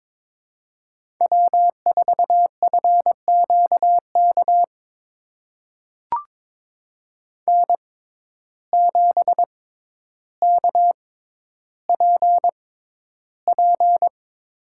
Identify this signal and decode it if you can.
Morse code, heard as a keyed tone, about 22 wpm, spaced as for 11 wpm, 700 Hz. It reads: W4FQK N7KPP
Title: USA Call Signs - 5 Characters - 6x Character Spacing - 22wpm